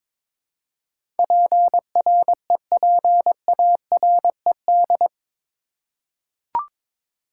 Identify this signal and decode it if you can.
Morse code, heard as a keyed tone, about 22 wpm, 700 Hz. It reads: PREPARED
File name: practice / 22wpm / Eight Letter Words - 22wpm